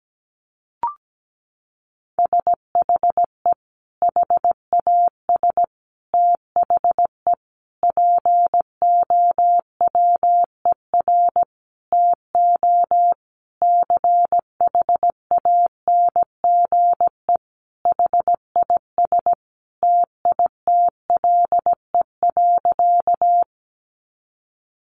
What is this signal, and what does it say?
Morse code, heard as a keyed tone, about 17 wpm, 700 Hz. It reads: SHE HAS THE POWER TO CHANGE HIS TITLE.
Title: Sentences from Top 1200 Words 17wpm